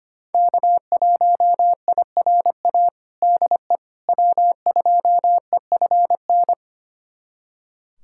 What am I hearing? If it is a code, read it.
Morse code, heard as a keyed tone, about 25 wpm, 700 Hz. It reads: K1IRA DE W2EFN
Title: USA Callsign DE Callsign - 5 Characters - 25wpm